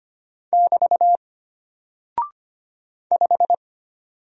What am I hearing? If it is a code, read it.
Morse code, heard as a keyed tone, about 25 wpm, 700 Hz. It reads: <BT> 5
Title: Single Character (Letter:Number:Punctuation:Prosign) Alt 25wpm